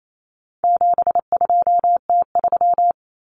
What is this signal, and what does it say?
Morse code, heard as a keyed tone, about 28 wpm, 700 Hz. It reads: 72T3
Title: Common Cut Numbers - 4-Digit 28wpm